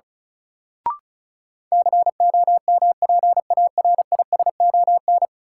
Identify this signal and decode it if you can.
Morse code, heard as a keyed tone, about 36 wpm, 700 Hz. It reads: COMPARISON
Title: Sentences from Top 1600 Words - New Word Drill Alt - 35wpm